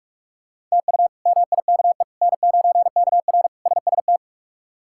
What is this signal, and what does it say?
Morse code, heard as a keyed tone, about 46 wpm, 700 Hz. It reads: TU MIKE N9KR SST
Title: K1USN SST QSO Practice - 45wpm